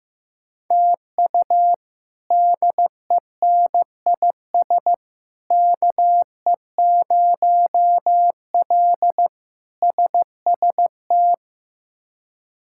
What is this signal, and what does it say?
Morse code, heard as a keyed tone, about 15 wpm, 700 Hz. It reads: TU DENIS KE0L SST
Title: K1USN SST QSO Practice - 15wpm